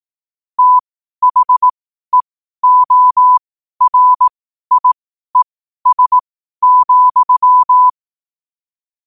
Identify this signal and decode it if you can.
Morse code, heard as a keyed tone, about 18 wpm, spaced as for 12 wpm, 1000 Hz. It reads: THEORIES,
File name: qotdfiles / 12wpm